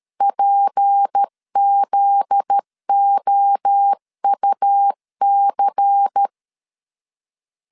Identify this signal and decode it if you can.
Morse code, heard as a keyed tone, about 13 wpm, 800 Hz. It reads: PZOUC
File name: LEZIONI / 10 SIGNS / 10B INT 12wpm 2414